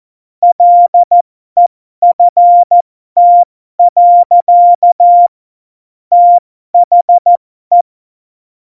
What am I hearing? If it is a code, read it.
Morse code, heard as a keyed tone, about 14 wpm, spaced as for 12 wpm, 700 Hz. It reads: LEFT. THE